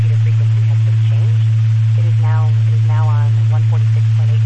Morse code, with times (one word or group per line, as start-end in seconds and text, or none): none